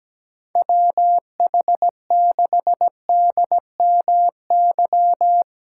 Morse code, heard as a keyed tone, about 17 wpm, 700 Hz. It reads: WH6DMY
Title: USA Callsign DE Callsign - 6 Characters - 17wpm